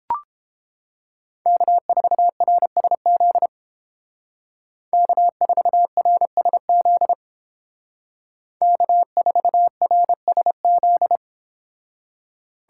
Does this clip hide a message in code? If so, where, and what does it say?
Yes: Morse code from 1.5 to 11.2 s, heard as a keyed tone, about 29 wpm, 700 Hz. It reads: K4RSZ K4RSZ K4RSZ